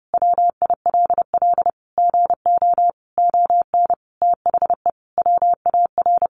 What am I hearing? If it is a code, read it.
Morse code, heard as a keyed tone, about 30 wpm, 700 Hz. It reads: WILL GO ON THE WAR
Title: Sentences from Top 1400 Words 30wpm